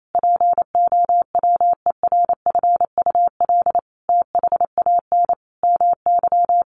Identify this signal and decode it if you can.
Morse code, heard as a keyed tone, about 28 wpm, 700 Hz. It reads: POWERFUL THAN MY
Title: Sentences from Top 1300 Words 28wpm